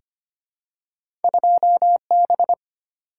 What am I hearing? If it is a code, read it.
Morse code, heard as a keyed tone, about 25 wpm, 700 Hz. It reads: 2B
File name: Two Letter-Number Combinations - 25wpm